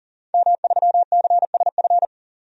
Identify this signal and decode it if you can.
Morse code, heard as a keyed tone, about 40 wpm, 700 Hz. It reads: M3CSF
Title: DXCC-100 Call Signs - 5 Characters - 40wpm